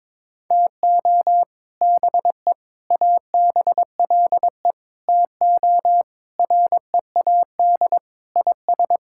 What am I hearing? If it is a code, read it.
Morse code, heard as a keyed tone, about 22 wpm, 700 Hz. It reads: TO BE ABLE TO READ IS
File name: practice / 22wpm / Sentences with 4-Word Phrases - 22wpm